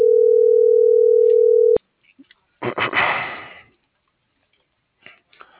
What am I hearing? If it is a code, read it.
Ringback tone (North America), 440 + 480 Hz.